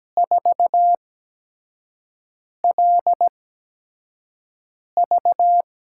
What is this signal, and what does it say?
Morse code, heard as a keyed tone, about 17 wpm, spaced as for 5 wpm, 700 Hz. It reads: 4LV
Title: USA Call Signs - 6 Characters - 8x Character Spacing - 17wpm